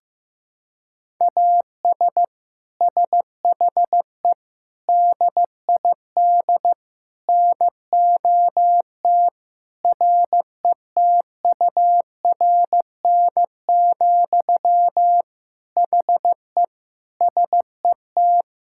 Morse code, heard as a keyed tone, about 15 wpm, 700 Hz. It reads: AS SHE DID NOT RETURN, HE SET
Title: Sentences from Top 1000 Words - Review Part 5 - 15wpm